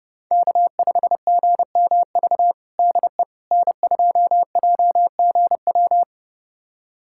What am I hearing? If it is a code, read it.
Morse code, heard as a keyed tone, about 30 wpm, 700 Hz. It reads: K5GMV DE N2JGW
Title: USA Callsign DE Callsign - 5 Characters - 30wpm